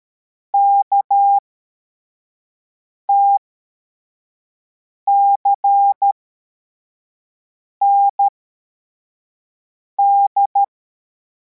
Morse code, heard as a keyed tone, about 13 wpm, spaced as for 4 wpm, 800 Hz. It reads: KTCND
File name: formation / cw / Text34